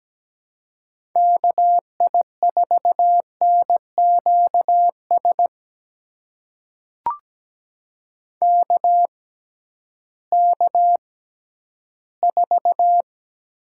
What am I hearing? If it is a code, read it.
Morse code, heard as a keyed tone, about 17 wpm, spaced as for 11 wpm, 700 Hz. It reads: KI4NQS KK4
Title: USA Call Signs - 6x Character Spacing - 17wpm